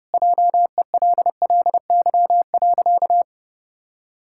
Morse code, heard as a keyed tone, about 32 wpm, 700 Hz. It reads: JELLY.
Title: Pangram Challenge v2 - 30wpm